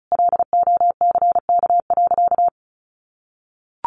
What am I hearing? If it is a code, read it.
Morse code, heard as a keyed tone, about 34 wpm, 700 Hz. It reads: LOCK.
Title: Sentences from Top 1300 Words 35wpm